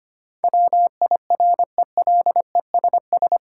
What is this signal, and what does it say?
Morse code, heard as a keyed tone, about 25 wpm, 700 Hz. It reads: WIRELESS
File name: practice / 25wpm / Difficult Words with Hints - 25wpm